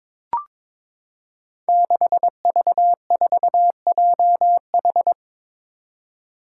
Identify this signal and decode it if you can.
Morse code, heard as a keyed tone, about 22 wpm, 700 Hz. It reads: 6V4JH
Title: Five Challenging Letter-Number Combinations 22wpm